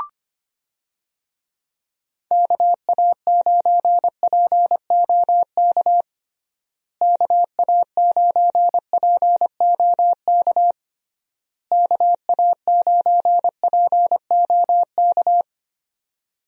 Morse code, heard as a keyed tone, about 25 wpm, 700 Hz. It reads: KA9POK KA9POK KA9POK
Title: USA Call Signs - 6 Characters - Repeated 3x - 25wpm